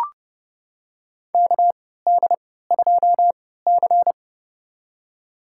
Morse code, heard as a keyed tone, about 29 wpm, spaced as for 17 wpm, 700 Hz. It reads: KD2C
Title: USA Call Signs - 4 Characters - 3x Character Spacing - 30wpm